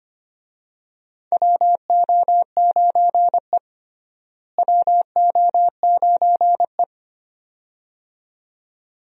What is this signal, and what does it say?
Morse code, heard as a keyed tone, about 25 wpm, 700 Hz. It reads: WO9E WO9E
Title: USA Call Signs - 4 Characters - Repeated 2x - 25wpm